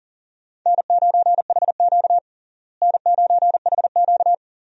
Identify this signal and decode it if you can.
Morse code, heard as a keyed tone, about 38 wpm, 700 Hz. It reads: N9HQ N9HQ
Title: USA Call Signs - 4 Characters - Repeated 2x - 40wpm